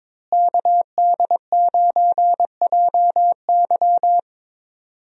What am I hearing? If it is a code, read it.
Morse code, heard as a keyed tone, about 22 wpm, 700 Hz. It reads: KD9JY
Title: USA Call Signs - 5 Characters - Repeated 2x - 22wpm